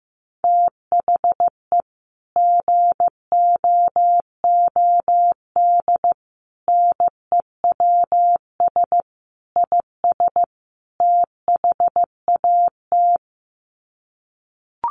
Morse code, heard as a keyed tone, about 15 wpm, 700 Hz. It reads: THE GOOD NEWS IS THAT
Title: Sets of 5 Words 15wpm